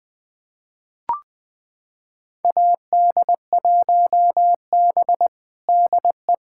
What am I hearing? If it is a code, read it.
Morse code, heard as a keyed tone, about 20 wpm, 700 Hz. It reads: AD1B DE
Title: USA Callsign DE Callsign - 4 Characters - 20wpm